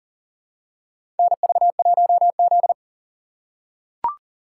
Morse code, heard as a keyed tone, about 40 wpm, 700 Hz. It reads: NV1Z
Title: USA Call Signs - 4 Characters - 6x Character Spacing - 40wpm